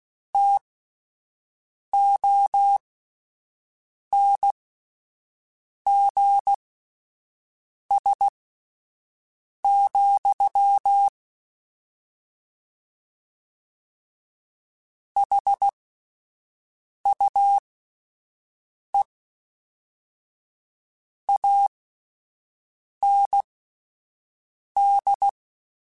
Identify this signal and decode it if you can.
Morse code, heard as a keyed tone, about 16 wpm, spaced as for 5 wpm, 800 Hz. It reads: TONGS, HUE AND
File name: Misc / Misc-mp3s / Word Pairs @5